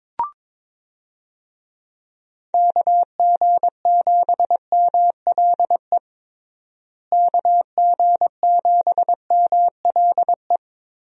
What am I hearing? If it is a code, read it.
Morse code, heard as a keyed tone, about 22 wpm, 700 Hz. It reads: KG7MLE KG7MLE
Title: USA Call Signs - 6 Characters - Repeated 2x - 22wpm